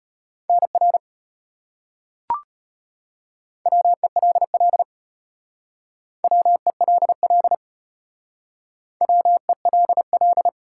Morse code, heard as a keyed tone, about 33 wpm, 700 Hz. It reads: NR WELL WELL WELL